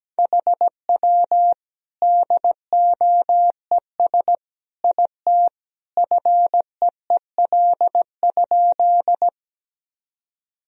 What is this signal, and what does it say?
Morse code, heard as a keyed tone, about 17 wpm, 700 Hz. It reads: HW DOES IT FEEL?
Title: Sentences from Top 400 Words and Abbreviations 17wpm